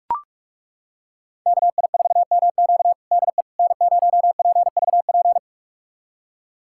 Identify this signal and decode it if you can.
Morse code, heard as a keyed tone, about 41 wpm, 700 Hz. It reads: KI4MQ DE N0PVP
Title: USA Callsign DE Callsign - 5 Characters - 45wpm